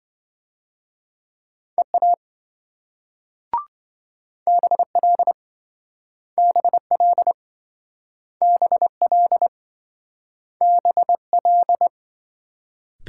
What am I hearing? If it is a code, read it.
Morse code, heard as a keyed tone, about 27 wpm, 700 Hz. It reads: EA BL BL BL BL